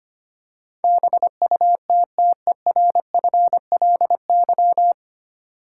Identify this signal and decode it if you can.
Morse code, heard as a keyed tone, about 25 wpm, 690 Hz. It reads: BUTTERFLY
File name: practice / 25wpm / Easy Words with Hints - 25wpm